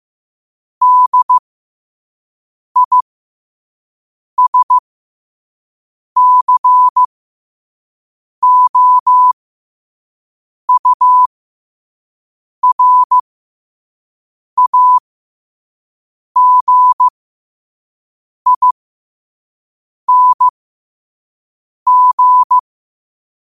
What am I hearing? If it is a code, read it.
Morse code, heard as a keyed tone, about 15 wpm, spaced as for 5 wpm, 1000 Hz. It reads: DISCOURAGING